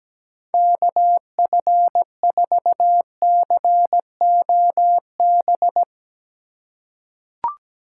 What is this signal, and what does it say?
Morse code, heard as a keyed tone, about 17 wpm, 700 Hz. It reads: KF4COB
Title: USA Call Signs - 6 Characters - 3x Character Spacing - 17wpm